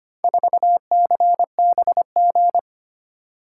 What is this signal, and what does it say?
Morse code, heard as a keyed tone, about 25 wpm, 700 Hz. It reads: 4CBG
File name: practice / 25wpm / Four Letter-Number Combinations - 25wpm